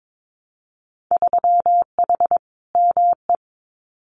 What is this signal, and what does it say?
Morse code, heard as a keyed tone, about 22 wpm, 700 Hz. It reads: 3H ME
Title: Winter Field Day Exchanges Encore 22wpm